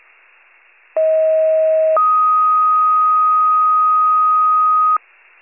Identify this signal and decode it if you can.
Two-tone page, heard as a one-second tone, then a three-second tone, about 640 then 1250 Hz. It is heard from 1.0 to 5.0 s.
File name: pager / 46 RESCUE / wcalert